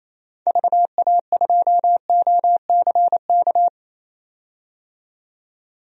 Morse code, heard as a keyed tone, about 28 wpm, 700 Hz. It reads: VA2OCK
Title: DXCC-100 Call Signs - 6 Characters - 28wpm